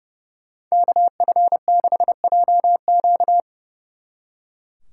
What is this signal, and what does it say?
Morse code, heard as a keyed tone, about 38 wpm, 700 Hz. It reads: KF6JQ